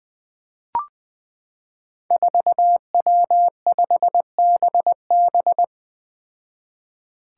Morse code, heard as a keyed tone, about 20 wpm, 700 Hz. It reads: 4W5BB
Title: Five Challenging Letter-Number Combinations 20wpm